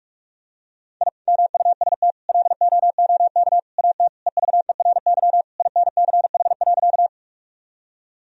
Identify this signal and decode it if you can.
Morse code, heard as a keyed tone, about 46 wpm, 700 Hz. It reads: I MUST LOOK AT EVERY INCH.
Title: Sentences from Top 1000 Words - Review Part 3 - 45wpm